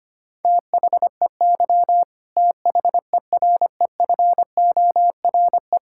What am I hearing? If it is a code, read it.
Morse code, heard as a keyed tone, about 25 wpm, 700 Hz. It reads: THEY THEREFORE
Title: Sentences from Top 1000 Words - Review Part 5 - 25wpm